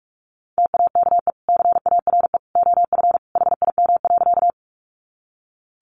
Morse code, heard as a keyed tone, about 44 wpm, 700 Hz. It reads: TAKE CARE OF HIM.
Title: Sentences from Top 300 Words 45wpm